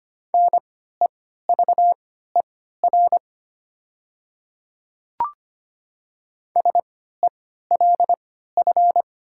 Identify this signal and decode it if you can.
Morse code, heard as a keyed tone, about 25 wpm, spaced as for 14 wpm, 700 Hz. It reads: NEVER SELF
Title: 3x Character Spacing - Top 500 Words - 25wpm